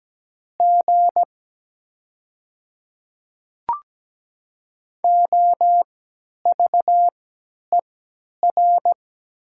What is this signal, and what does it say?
Morse code, heard as a keyed tone, about 17 wpm, spaced as for 10 wpm, 700 Hz. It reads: G OVER
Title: 3x Character Spacing - Top 100 Words - 17wpm